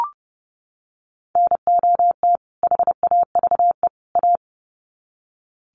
Morse code, heard as a keyed tone, about 30 wpm, 700 Hz. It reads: NOT HAVE A
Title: Sets of 3 Words 30wpm